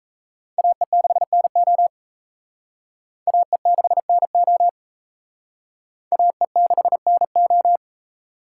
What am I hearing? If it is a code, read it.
Morse code, heard as a keyed tone, about 35 wpm, 700 Hz. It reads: AE6NO AE6NO AE6NO
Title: Speed Racing - USA 2x2 Call Signs 28wpm